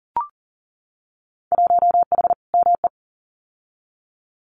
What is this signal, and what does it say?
Morse code, heard as a keyed tone, about 40 wpm, 700 Hz. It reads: 1H ME
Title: Winter Field Day Exchanges 40wpm